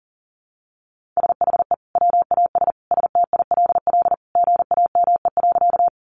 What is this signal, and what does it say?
Morse code, heard as a keyed tone, about 38 wpm, 700 Hz. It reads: SHE WAS STILL GAME.
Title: Sentences from Top 500 Words 40wpm